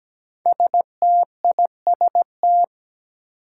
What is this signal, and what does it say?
Morse code, heard as a keyed tone, about 17 wpm, 700 Hz. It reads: STIST
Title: Five Challenging Letter-Number Combinations 17wpm